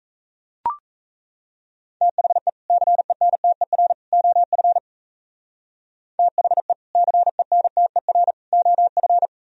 Morse code, heard as a keyed tone, about 35 wpm, 700 Hz. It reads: THE CENTER OF THE CENTER OF